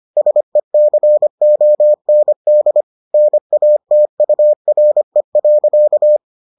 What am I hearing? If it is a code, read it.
Morse code, heard as a keyed tone, about 25 wpm, 600 Hz. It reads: SECOND NATURE.